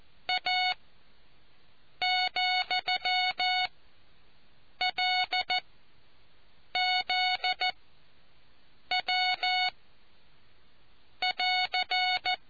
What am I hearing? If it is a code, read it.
Morse code, heard as a keyed tone, about 14 wpm, spaced as for 6 wpm, 740 Hz. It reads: A,LZW<AR>